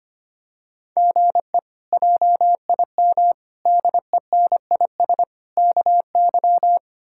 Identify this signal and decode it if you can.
Morse code, heard as a keyed tone, about 25 wpm, 700 Hz. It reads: GE JIM DENIS KY
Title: K1USN SST QSO Practice - 25wpm